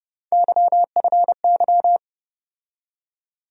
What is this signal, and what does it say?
Morse code, heard as a keyed tone, about 30 wpm, 700 Hz. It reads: YFY
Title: USA Call Sign Postfixes - 3 Letters 30wpm